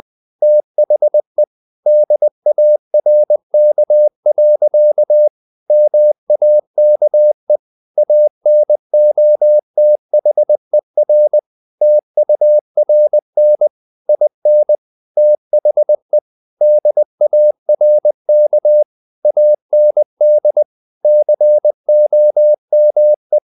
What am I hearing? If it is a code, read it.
Morse code, heard as a keyed tone, about 20 wpm, 600 Hz. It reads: THE DARK. MAKE ANOTHER TURN IN THE DARK AND COME
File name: the old man and the sea / the old man and the sea part 1 20wpm